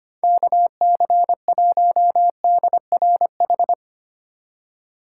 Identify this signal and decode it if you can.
Morse code, heard as a keyed tone, about 25 wpm, 700 Hz. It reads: KC1DRH
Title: USA Call Signs - 6 Characters - 3x Character Spacing - 25wpm